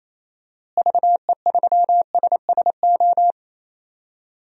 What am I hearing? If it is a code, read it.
Morse code, heard as a keyed tone, about 28 wpm, 700 Hz. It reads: VE3SSO
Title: DXCC-100 Call Signs - 6 Characters - Repeated 3x - 28wpm